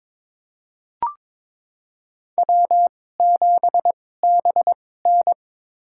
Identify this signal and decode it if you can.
Morse code, heard as a keyed tone, about 22 wpm, spaced as for 16 wpm, 700 Hz. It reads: W7BN
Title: USA Call Signs - 4 Characters - 2x Character Spacing - 22wpm